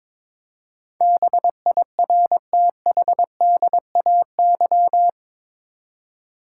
Morse code, heard as a keyed tone, about 22 wpm, 700 Hz. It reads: BIRTHDAY
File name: practice / 22wpm / Eight Letter Words - 22wpm